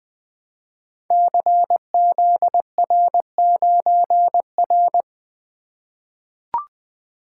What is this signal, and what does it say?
Morse code, heard as a keyed tone, about 20 wpm, 700 Hz. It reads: CZR9R
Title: Code Groups - 5 Letter:Numbers - 20wpm